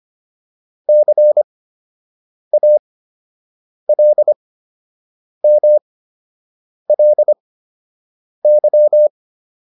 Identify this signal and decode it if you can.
Morse code, heard as a keyed tone, about 25 wpm, spaced as for 7 wpm, 600 Hz. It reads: CALMLY
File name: LY6 07